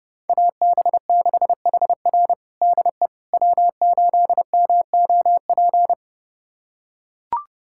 Morse code, heard as a keyed tone, about 30 wpm, 700 Hz. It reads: AB6HR DE W8MOP
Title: USA Callsign DE Callsign - 5 Characters - 30wpm